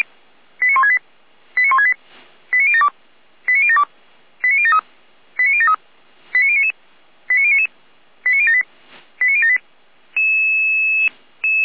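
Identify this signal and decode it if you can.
Five-tone selective call (ZVEI), heard as quick runs of five tones, tones of about 70 ms. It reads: 89168 89168 89072 89072 89073 89073 89099 89099 89078 89078
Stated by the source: ZVEI five-tone call 89150 received on SDR.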